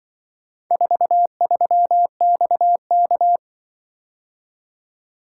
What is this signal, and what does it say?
Morse code, heard as a keyed tone, about 23 wpm, 700 Hz. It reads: 43XK
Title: Speed-Racing - Four Letter-Number Combinations - 20wpm